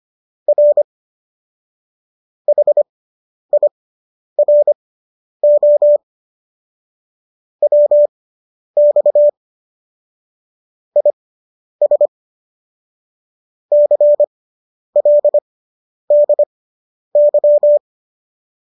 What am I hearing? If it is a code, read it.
Morse code, heard as a keyed tone, about 25 wpm, spaced as for 10 wpm, 590 Hz. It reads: R HIRO WX IS CLDY
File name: qso005 10